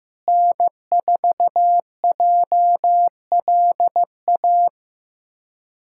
Morse code, heard as a keyed tone, about 15 wpm, 700 Hz. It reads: N4JLA
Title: USA Callsign DE Callsign - 5 Characters - 15wpm